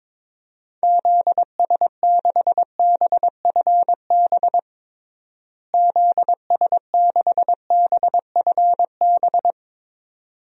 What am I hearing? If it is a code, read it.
Morse code, heard as a keyed tone, about 22 wpm, 700 Hz. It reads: ZS6BFB ZS6BFB
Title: DXCC-100 Call Signs - 6 Characters - Repeated 2x - 22wpm